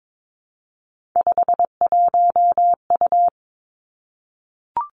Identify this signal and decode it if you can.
Morse code, heard as a keyed tone, about 22 wpm, 700 Hz. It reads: 51U
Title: Three Challenging Letter-Number Combinations 22wpm